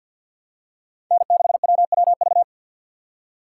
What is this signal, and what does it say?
Morse code, heard as a keyed tone, about 50 wpm, 700 Hz. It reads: N6WWV
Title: USA Call Signs - 5 Characters - 4x Character Spacing - 50wpm